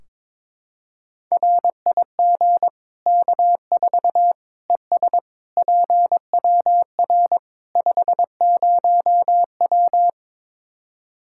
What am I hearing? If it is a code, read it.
Morse code, heard as a keyed tone, about 22 wpm, 700 Hz. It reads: RIG K4 ES PWR 50W
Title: Practice QSO - Exchange 2 - Info - 22wpm